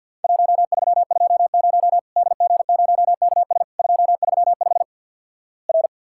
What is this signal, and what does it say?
Morse code, heard as a keyed tone, about 50 wpm, 700 Hz. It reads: 1320 DG0KS 235 R
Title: Worked All Europe DX Contest - QTC Exchange - 50wpm